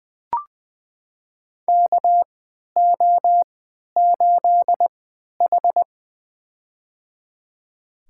Morse code, heard as a keyed tone, about 20 wpm, spaced as for 11 wpm, 700 Hz. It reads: KO8H
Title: USA Call Signs - 4 Characters - 3x Character Spacing - 20wpm